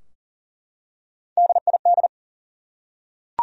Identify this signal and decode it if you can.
Morse code, heard as a keyed tone, about 40 wpm, 700 Hz. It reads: DID
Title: Top 100 Words Repeated 40wpm